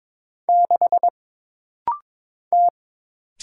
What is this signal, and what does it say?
Morse code, heard as a keyed tone, about 22 wpm, 700 Hz. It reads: 6 T
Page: This rapid-fire timing is geared to push your ICR (Instant Character Recognition) skills.